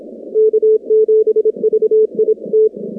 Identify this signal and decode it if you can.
Morse code, heard as a keyed tone, about 27 wpm, 440 Hz. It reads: K7VIT
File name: contests / 2019-cw-wpx-kt5j-k5pi- -2330 0-19